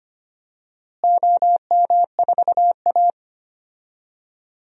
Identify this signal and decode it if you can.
Morse code, heard as a keyed tone, about 25 wpm, 700 Hz. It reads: OM4A
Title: DXCC-100 Call Signs - 4 Characters - Repeated 2x - 25wpm